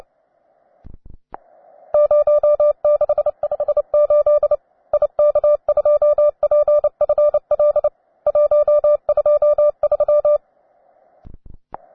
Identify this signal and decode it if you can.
Morse code, heard as a keyed tone, about 29 wpm, 590 Hz. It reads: 0658 IK2PFL 123
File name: wae / cw beginner / ua9am cw